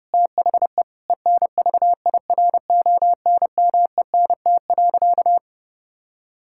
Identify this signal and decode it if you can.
Morse code, heard as a keyed tone, about 30 wpm, 700 Hz. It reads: THE ENVIRONMENT.
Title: Sentences from Top 1000 Words - Review Part 6 - 30wpm